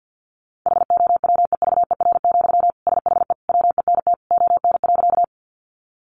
Morse code, heard as a keyed tone, about 49 wpm, 700 Hz. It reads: HOWEVER, SHE WENT ON.
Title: Sentences from Top 600 Words 50wpm